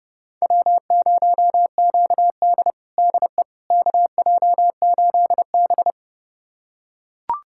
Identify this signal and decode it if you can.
Morse code, heard as a keyed tone, about 32 wpm, 700 Hz. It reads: W0QD DE KJ8B
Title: USA Callsign DE Callsign - 4 Characters - 30wpm